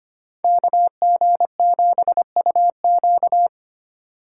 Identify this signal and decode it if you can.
Morse code, heard as a keyed tone, about 31 wpm, 700 Hz. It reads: KG7UQ